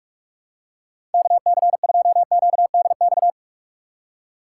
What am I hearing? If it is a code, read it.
Morse code, heard as a keyed tone, about 44 wpm, 690 Hz. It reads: KC2QDX
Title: USA Call Signs - 4x Character Spacing - 45wpm